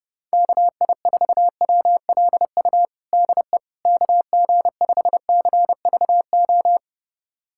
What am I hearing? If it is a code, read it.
Morse code, heard as a keyed tone, about 30 wpm, 700 Hz. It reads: KI4WLU DE KG5CVO
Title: USA Callsign DE Callsign - 6 Characters - 30wpm